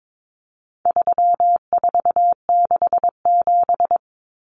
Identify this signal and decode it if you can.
Morse code, heard as a keyed tone, about 22 wpm, 700 Hz. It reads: 3467
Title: Numbers - 4-Digit 22wpm